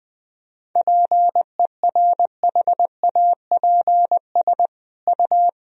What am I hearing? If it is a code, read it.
Morse code, heard as a keyed tone, about 20 wpm, 700 Hz. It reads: PERHAPS U
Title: Sentences from Top 500 Words and Abbreviations 20wpm